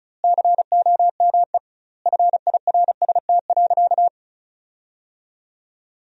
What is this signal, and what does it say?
Morse code, heard as a keyed tone, about 34 wpm, 700 Hz. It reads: COME FIRST.
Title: 2x Word-Spacing - Sentences from Top 300 Words - 35wpm